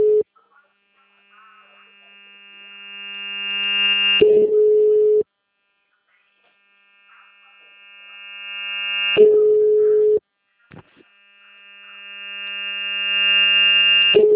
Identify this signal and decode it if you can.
Ringback tone (Europe, 425 Hz).